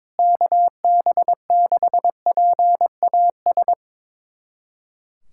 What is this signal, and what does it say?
Morse code, heard as a keyed tone, about 22 wpm, 700 Hz. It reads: KB6PAS
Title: USA Callsign DE Callsign 22wpm